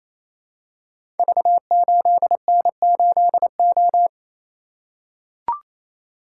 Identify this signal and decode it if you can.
Morse code, heard as a keyed tone, about 28 wpm, 700 Hz. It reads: V8N8O
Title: Code Groups - 5 Letter:Numbers - 28wpm